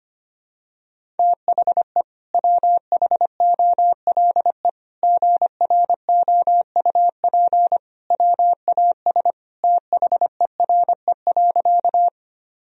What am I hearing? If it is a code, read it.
Morse code, heard as a keyed tone, about 25 wpm, 700 Hz. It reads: THE WHOLE GROUP WAS THERE.
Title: Sentences from Top 1000 Words - Review Part 1 - 25wpm